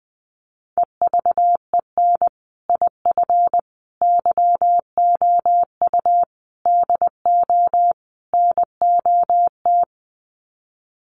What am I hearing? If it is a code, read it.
Morse code, heard as a keyed tone, about 20 wpm, 700 Hz. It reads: EVEN IF YOU DO NOT
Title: Sets of 5 Words Encore 20wpm